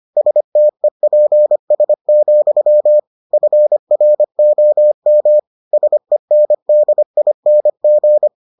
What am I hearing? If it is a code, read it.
Morse code, heard as a keyed tone, about 25 wpm, 600 Hz. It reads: STEPS, FROM SENDING